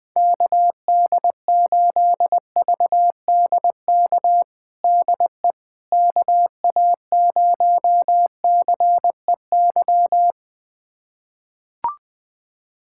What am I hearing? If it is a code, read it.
Morse code, heard as a keyed tone, about 20 wpm, 700 Hz. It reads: KD8VDK DE KA0CEY
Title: USA Callsign DE Callsign 20wpm